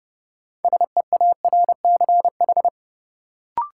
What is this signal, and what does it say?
Morse code, heard as a keyed tone, about 30 wpm, 700 Hz. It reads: SEARCH
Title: Sentences from Top 800 Words - New Word Drill - 30wpm